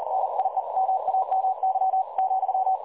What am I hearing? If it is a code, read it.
Morse code, heard as a keyed tone, about 40 wpm, 760 Hz. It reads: NIE MAMY ,
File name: dlazaawansowanych / nagraniazpasma / websdr recording 3565.0kHz